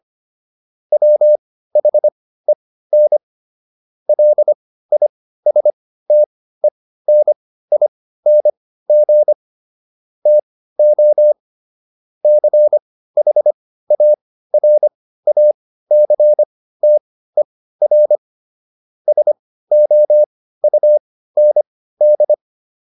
Morse code, heard as a keyed tone, about 25 wpm, spaced as for 15 wpm, 600 Hz. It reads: WHEN LISTENING TO CHARACTER SOUND